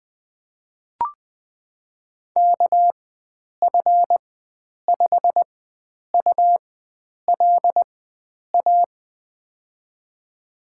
Morse code, heard as a keyed tone, about 20 wpm, spaced as for 9 wpm, 700 Hz. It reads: KF5ULA
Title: USA Call Signs - 6 Characters - 4x Character Spacing - 20wpm